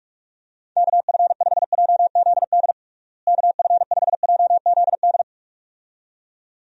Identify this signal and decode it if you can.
Morse code, heard as a keyed tone, about 46 wpm, 700 Hz. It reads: KF5JZD KF5JZD
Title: USA Call Signs - 6 Characters - Repeated 2x - 45wpm